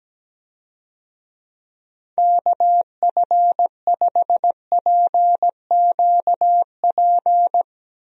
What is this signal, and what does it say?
Morse code, heard as a keyed tone, about 17 wpm, 700 Hz. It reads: KF5PQP
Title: USA Call Signs - 6 Characters - Repeated 2x - 17wpm